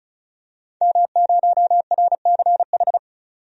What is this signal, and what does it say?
Morse code, heard as a keyed tone, about 34 wpm, 700 Hz. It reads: M0RCH
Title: DXCC-100 Call Signs - 5 Characters - 35wpm